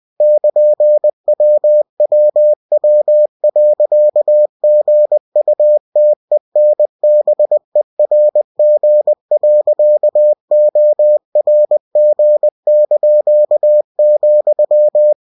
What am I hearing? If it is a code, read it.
Morse code, heard as a keyed tone, about 20 wpm, 600 Hz. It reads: <KN>WWW.GUTENBERG.ORG),
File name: cw-books / winnie-the-pooh / winnie-20wpm-page-0116